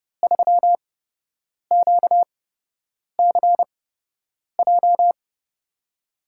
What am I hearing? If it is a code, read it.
Morse code, heard as a keyed tone, about 31 wpm, spaced as for 8 wpm, 700 Hz. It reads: 3QCJ
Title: USA Call Signs - 5 Characters - 8x Character Spacing - 30wpm